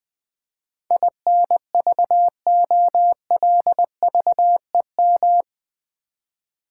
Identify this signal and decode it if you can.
Morse code, heard as a keyed tone, about 20 wpm, 700 Hz. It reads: INVOLVEM
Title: Build A Word - 4 Letters At A Time - 11-Letter Words - 20wpm